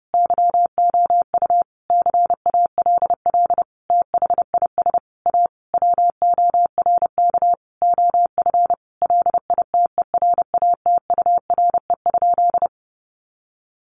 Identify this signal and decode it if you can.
Morse code, heard as a keyed tone, about 28 wpm, 700 Hz. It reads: YOU CALL THIS A WORK OF LITERATURE?
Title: Sentences from Top 600 Words 30wpm